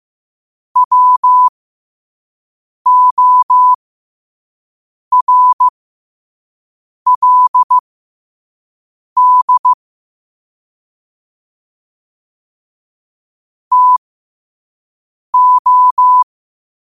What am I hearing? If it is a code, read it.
Morse code, heard as a keyed tone, about 15 wpm, spaced as for 5 wpm, 1000 Hz. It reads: WORLD TO